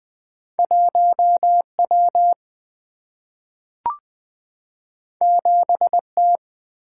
Morse code, heard as a keyed tone, about 20 wpm, 700 Hz. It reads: 1 W 7 T
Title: Single-Letter Single-Number 20wpm